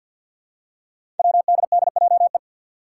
Morse code, heard as a keyed tone, about 47 wpm, 700 Hz. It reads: WDDJE
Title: Five Challenging Letter-Number Combinations 50wpm